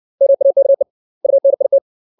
Morse code, heard as a keyed tone, about 60 wpm, 550 Hz. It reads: MAKE UNIT
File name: 632 words / 632 Words 60WPM